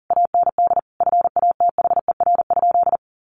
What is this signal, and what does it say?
Morse code, heard as a keyed tone, about 40 wpm, 700 Hz. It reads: AND FATHER?
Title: Sentences from Top 1200 Words 40wpm